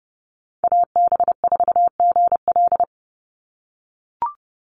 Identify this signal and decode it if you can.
Morse code, heard as a keyed tone, about 30 wpm, 700 Hz. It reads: AB4GL
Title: USA Call Signs 30wpm